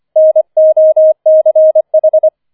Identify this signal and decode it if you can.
Morse code, heard as a keyed tone, about 25 wpm, 630 Hz. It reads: NOCH
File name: 25wpm